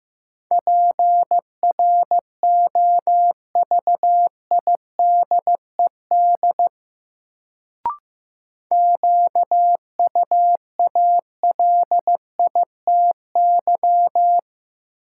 Morse code, heard as a keyed tone, about 15 wpm, 700 Hz. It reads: PROVIDED QUALITY
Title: Sentences from Top 1500 Words - New Word Drill Alt - 15wpm